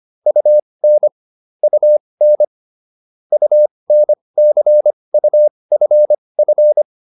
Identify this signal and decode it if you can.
Morse code, heard as a keyed tone, about 25 wpm, spaced as for 20 wpm, 600 Hz. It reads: UN UN UNCUFF
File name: UN6 20